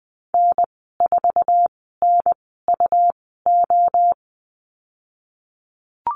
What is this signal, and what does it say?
Morse code, heard as a keyed tone, about 20 wpm, spaced as for 15 wpm, 700 Hz. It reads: N4NUO
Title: 140 Lesson 23g - Letter f - Call Signs 20wpm